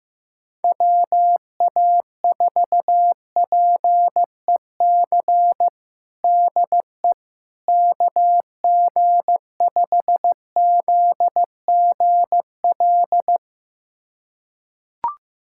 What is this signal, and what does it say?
Morse code, heard as a keyed tone, about 15 wpm, 700 Hz. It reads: WA4PEC DE KG5ZGL
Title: USA Callsign DE Callsign - 6 Characters - 15wpm